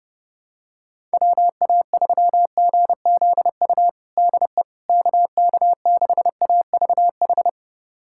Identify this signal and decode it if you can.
Morse code, heard as a keyed tone, about 32 wpm, 700 Hz. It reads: WA3GZU DE KK6AVH
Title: USA Callsign DE Callsign - 6 Characters - 30wpm